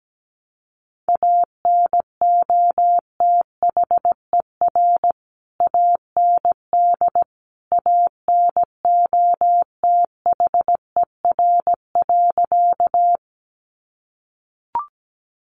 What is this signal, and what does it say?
Morse code, heard as a keyed tone, about 17 wpm, 700 Hz. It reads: ANOTHER AND ANOTHER.
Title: Sentences from Top 600 Words 17wpm